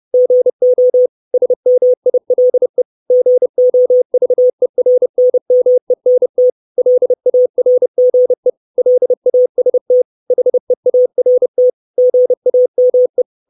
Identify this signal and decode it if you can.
Morse code, heard as a keyed tone, about 30 wpm, 500 Hz. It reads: GO SMILE GOVERNMENT LARGE LAST HEART GAME
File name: 632 Words 30WPM